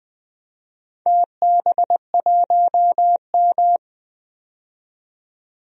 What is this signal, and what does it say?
Morse code, heard as a keyed tone, about 20 wpm, 700 Hz. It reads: TB1M
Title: DXCC-100 Call Signs - 4 Characters - 20wpm